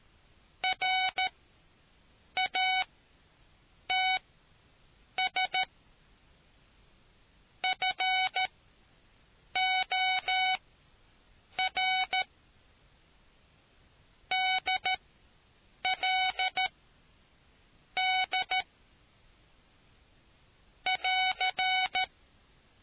Morse code, heard as a keyed tone, about 13 wpm, spaced as for 6 wpm, 750 Hz. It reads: RATS FOR DL D <AR>